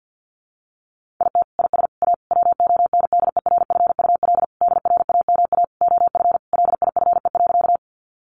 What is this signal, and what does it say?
Morse code, heard as a keyed tone, about 50 wpm, 700 Hz. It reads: IT IS A WONDERFUL DRAMA OF LIFE.
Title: Sentences from Top 1500 Words 50wpm